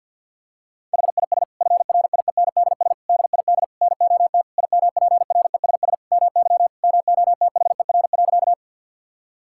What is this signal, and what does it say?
Morse code, heard as a keyed tone, about 50 wpm, 700 Hz. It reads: HIS FRIENDS DID NOT IMPRESS MY MOTHER.
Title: Sentences from Top 1600 Words 50wpm